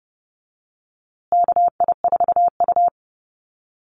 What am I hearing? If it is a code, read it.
Morse code, heard as a keyed tone, about 30 wpm, 700 Hz. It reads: KI4U
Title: USA Call Signs Repeated 3x 30wpm